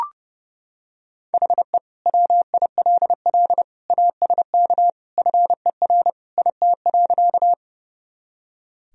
Morse code, heard as a keyed tone, about 30 wpm, 700 Hz. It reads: HE WILL ASK FER IT.
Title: Sentences from Top 200 Words and Abbreviations 30wpm